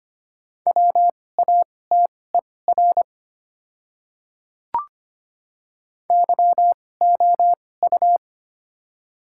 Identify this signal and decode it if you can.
Morse code, heard as a keyed tone, about 25 wpm, spaced as for 18 wpm, 700 Hz. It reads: WATER YOU
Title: 2x Character Spacing - Top 200 Words - 25wpm